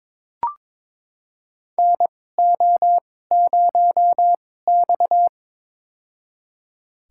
Morse code, heard as a keyed tone, about 22 wpm, spaced as for 16 wpm, 700 Hz. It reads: NO0X
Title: USA Call Signs - 4 Characters - 2x Character Spacing - 22wpm